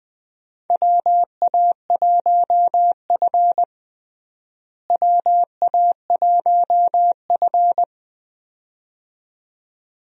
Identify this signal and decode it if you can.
Morse code, heard as a keyed tone, about 20 wpm, 700 Hz. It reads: WA1F WA1F
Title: USA Call Signs - 4 Characters - Repeated 2x - 20wpm